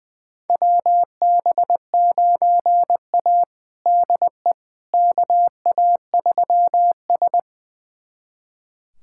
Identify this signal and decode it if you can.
Morse code, heard as a keyed tone, about 21 wpm, 700 Hz. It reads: WB9A DE KA3S
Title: USA Callsign DE Callsign - 4 Characters - 20wpm